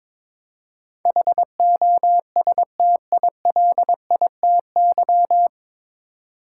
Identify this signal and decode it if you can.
Morse code, heard as a keyed tone, about 22 wpm, 700 Hz. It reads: HOSTILITY
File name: 9 Letter Words - First 3 Letters Given - 22wpm